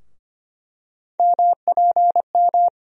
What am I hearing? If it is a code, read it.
Morse code, heard as a keyed tone, about 25 wpm, 700 Hz. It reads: MPM
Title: USA Call Sign Postfixes - 3 Letters 25wpm